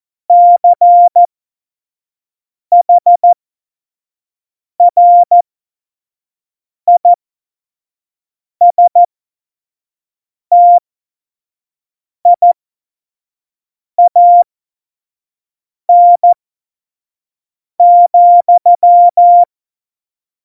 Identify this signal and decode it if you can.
Morse code, heard as a keyed tone, about 14 wpm, spaced as for 5 wpm, 700 Hz. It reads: CHRISTIAN,
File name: Morsekurs / Witze / Witz 19. September 17, Tempo 25